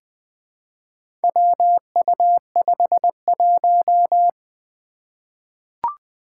Morse code, heard as a keyed tone, about 21 wpm, 700 Hz. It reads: WU51
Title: Four Challenging Letter-Number Combinations 20wpm